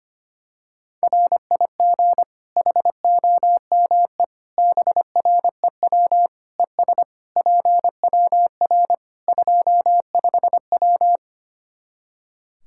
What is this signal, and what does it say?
Morse code, heard as a keyed tone, about 25 wpm, 700 Hz. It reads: RIG HOME BREW ES PWR 25W
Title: Practice QSO - Exchange 2 - Info - 25wpm